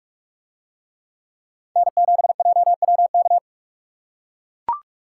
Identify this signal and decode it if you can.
Morse code, heard as a keyed tone, about 44 wpm, 700 Hz. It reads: N7JWK